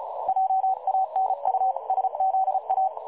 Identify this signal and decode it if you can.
Morse code, heard as a keyed tone, about 32 wpm, 760 Hz. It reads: JANUSZA
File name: dlazaawansowanych / nagraniazpasma / websdr recording 3565.0kHz